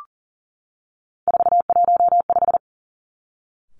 Morse code, heard as a keyed tone, about 42 wpm, 700 Hz. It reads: 415
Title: Three Challenging Letter-Number Combinations 40wpm